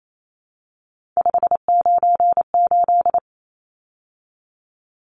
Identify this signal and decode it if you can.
Morse code, heard as a keyed tone, about 28 wpm, 700 Hz. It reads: 598
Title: Numbers - 3-Digit 28wpm